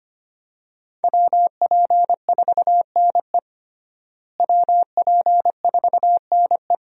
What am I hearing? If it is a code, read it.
Morse code, heard as a keyed tone, about 25 wpm, 700 Hz. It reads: WP4NE WP4NE
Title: USA Call Signs - 5 Characters - Repeated 2x - 25wpm